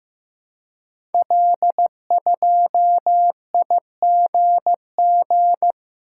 Morse code, heard as a keyed tone, about 15 wpm, 700 Hz. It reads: L2IGG
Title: Code Groups - 5 Letter:Numbers - 15wpm